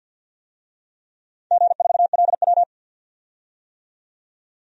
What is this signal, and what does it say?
Morse code, heard as a keyed tone, about 53 wpm, 700 Hz. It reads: G4LW
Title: DXCC-100 Call Signs - 4 Characters - 50wpm